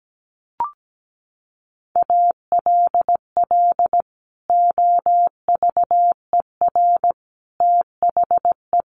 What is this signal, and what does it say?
Morse code, heard as a keyed tone, about 17 wpm, 700 Hz. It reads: ALL OVER THE
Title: Sets of 4 Words 17wpm